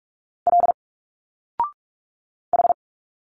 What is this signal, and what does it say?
Morse code, heard as a keyed tone, about 45 wpm, 700 Hz. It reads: L H
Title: Single Letters 45wpm